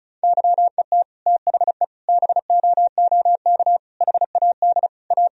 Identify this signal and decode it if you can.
Morse code, heard as a keyed tone, about 34 wpm, 700 Hz. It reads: YET THE BOOK HAD A
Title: Sentences from Top 1000 Words - Review Part 4 - 35wpm